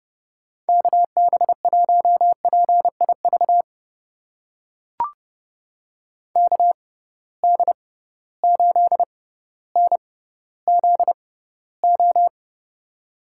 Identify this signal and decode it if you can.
Morse code, heard as a keyed tone, about 30 wpm, spaced as for 15 wpm, 700 Hz. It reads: KB1PIV KD8NZO
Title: USA Call Signs - 6 Characters - 6x Character Spacing - 30wpm